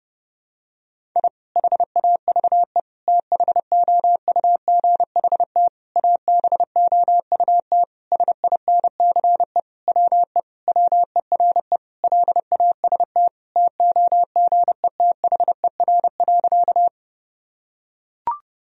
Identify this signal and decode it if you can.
Morse code, heard as a keyed tone, about 30 wpm, 700 Hz. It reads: I HAVE THOUGHT ABOUT SINCE WE WERE LAST TOGETHER.
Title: Sentences from Top 1000 Words - Review Part 1 - 30wpm